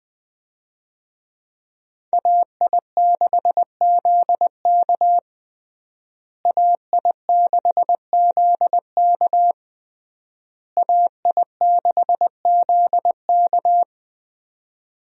Morse code, heard as a keyed tone, about 21 wpm, 700 Hz. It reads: AI6ZK AI6ZK AI6ZK
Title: USA Call Signs - 5 Characters - Repeated 3x - 20wpm